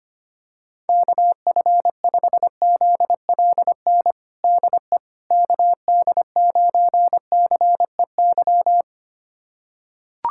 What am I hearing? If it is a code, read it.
Morse code, heard as a keyed tone, about 25 wpm, 700 Hz. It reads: KF5ZLN DE KD9CEY
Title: USA Callsign DE Callsign 25wpm